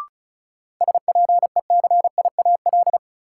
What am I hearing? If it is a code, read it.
Morse code, heard as a keyed tone, about 36 wpm, 700 Hz. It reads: SPECIAL
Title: Sentences from Top 1200 Words - New Word Drill Alt - 35wpm